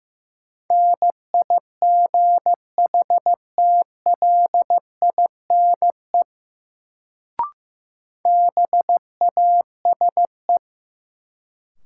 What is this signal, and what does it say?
Morse code, heard as a keyed tone, about 15 wpm, 700 Hz. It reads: NIGHTLINE BASE
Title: Sets of 5 Words - New Word Drill - 15wpm